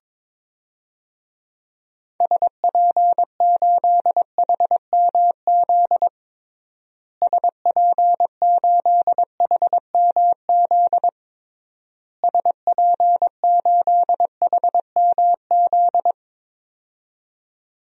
Morse code, heard as a keyed tone, about 22 wpm, 700 Hz. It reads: SP8HMZ SP8HMZ SP8HMZ
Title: DXCC-100 Call Signs - 6 Characters - Repeated 3x - 22wpm